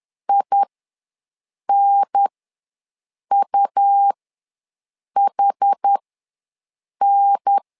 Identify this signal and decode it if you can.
Morse code, heard as a keyed tone, about 10 wpm, spaced as for 6 wpm, 800 Hz. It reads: INUHN
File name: 2 UHDN ws 10wpm